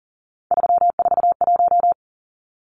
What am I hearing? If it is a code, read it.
Morse code, heard as a keyed tone, about 40 wpm, 700 Hz. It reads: 341
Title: Numbers - 3-Digit 40wpm